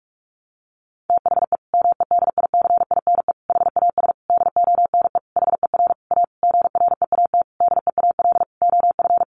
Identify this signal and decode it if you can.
Morse code, heard as a keyed tone, about 44 wpm, 700 Hz. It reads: THE MEDICINE HAS DONE HER A GREAT DEAL OF
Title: Sentences from Top 900 Words 45wpm